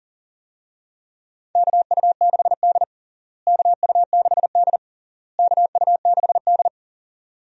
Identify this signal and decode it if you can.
Morse code, heard as a keyed tone, about 40 wpm, 700 Hz. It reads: KU6D KU6D KU6D
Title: USA Call Signs - 4 Characters - Repeated 3x - 40wpm